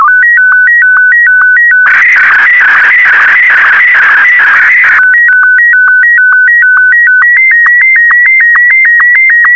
SSTV (slow-scan television), Martin 1.